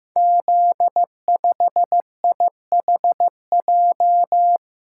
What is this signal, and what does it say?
Morse code, heard as a keyed tone, about 16 wpm, 700 Hz. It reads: Z5IHJ
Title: Code Groups - 5 Letter:Numbers - 15wpm